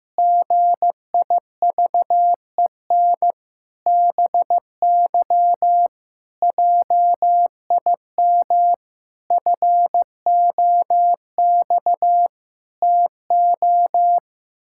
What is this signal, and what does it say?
Morse code, heard as a keyed tone, about 15 wpm, 700 Hz. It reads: GIVEN BY JIM FOX TO
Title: Pangram Challenge v2 - 15wpm